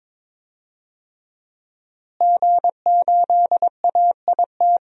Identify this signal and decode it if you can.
Morse code, heard as a keyed tone, about 22 wpm, 700 Hz. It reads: G8AIT
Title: DXCC-100 Call Signs - 5 Characters - Repeated 3x - 22wpm